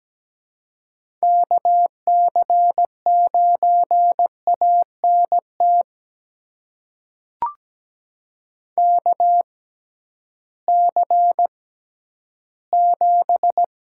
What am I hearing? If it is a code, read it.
Morse code, heard as a keyed tone, about 17 wpm, spaced as for 11 wpm, 700 Hz. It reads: KC9ANT KC7
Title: USA Call Signs - 6 Characters - 6x Character Spacing - 17wpm